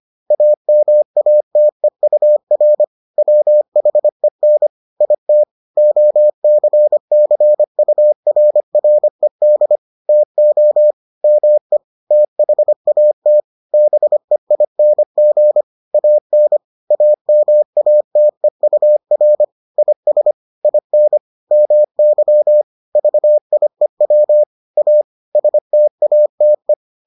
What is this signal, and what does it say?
Morse code, heard as a keyed tone, about 25 wpm, 600 Hz. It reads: AMATEUR WHEN IT OCCURRED TO ME THAT BEING AN AMATEUR IS IN MY VIEW A STATE